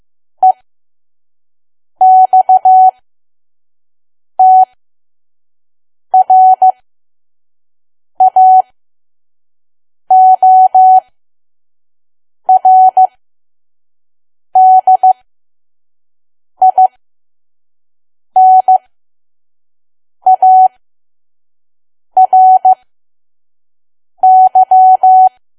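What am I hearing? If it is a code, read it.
Morse code, heard as a keyed tone, about 15 wpm, spaced as for 5 wpm, 750 Hz. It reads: EXTRAORDINARY
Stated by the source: Morse code practice at 5 WPM from KQ2P.COM.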